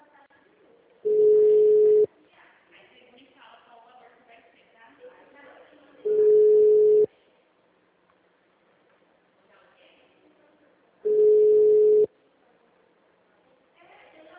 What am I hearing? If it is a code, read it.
Ringback tone (Europe, 425 Hz), with speech.
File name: out- -2007- - - .50780